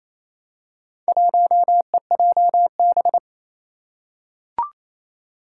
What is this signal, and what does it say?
Morse code, heard as a keyed tone, about 27 wpm, 700 Hz. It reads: 1EJB
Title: Four Challenging Letter-Number Combinations 28wpm